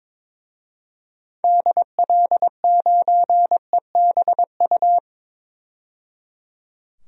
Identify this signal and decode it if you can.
Morse code, heard as a keyed tone, about 22 wpm, 700 Hz. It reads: DL9EBU
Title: DXCC-100 Call Signs - 6 Characters - 22wpm